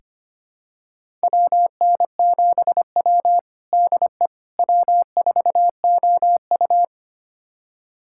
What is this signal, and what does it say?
Morse code, heard as a keyed tone, about 25 wpm, 700 Hz. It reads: WN7W DE W4OU
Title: USA Callsign DE Callsign - 4 Characters - 25wpm